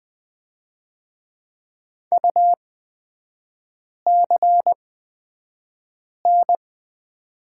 Morse code, heard as a keyed tone, about 20 wpm, spaced as for 5 wpm, 700 Hz. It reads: UCN
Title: 243 Lesson 39e - Letter z - Three Characters 20wpm